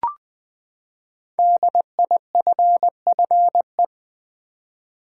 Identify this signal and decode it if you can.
Morse code, heard as a keyed tone, about 20 wpm, 700 Hz. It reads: DIFFE
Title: Build A Word - 5 Letters At A Time - 10-Letter Words - 20wpm